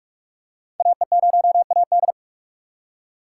Morse code, heard as a keyed tone, about 44 wpm, 690 Hz. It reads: AE0AD
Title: Code Groups - 5 Letter:Numbers - 45wpm